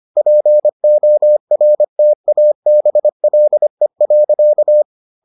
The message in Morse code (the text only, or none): PORTABLE.